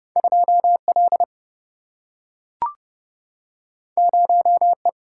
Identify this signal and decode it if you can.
Morse code, heard as a keyed tone, about 30 wpm, 700 Hz. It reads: 2L 0E
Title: Single-Letter Single-Number 30wpm